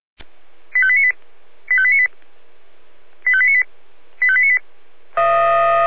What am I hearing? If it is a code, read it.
Five-tone selective call (ZVEI), heard as quick runs of five tones, tones of about 70 ms. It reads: 85898 85898 85898 85898